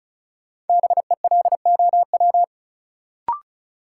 Morse code, heard as a keyed tone, about 34 wpm, 700 Hz. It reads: BELOW
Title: Sentences from Top 1000 Words - Review Part 5 - New Word Drill - 35wpm